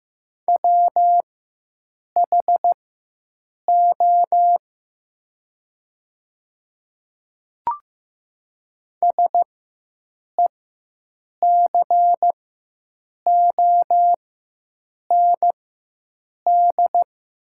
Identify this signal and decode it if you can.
Morse code, heard as a keyed tone, about 15 wpm, spaced as for 7 wpm, 700 Hz. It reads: WHO SECOND
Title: 4x Character Spacing - Top 300 Words - 15wpm